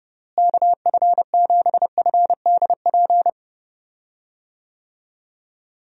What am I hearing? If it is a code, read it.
Morse code, heard as a keyed tone, about 32 wpm, 700 Hz. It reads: KF7FDP
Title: USA Call Signs - 6 Characters - Repeated 3x - 30wpm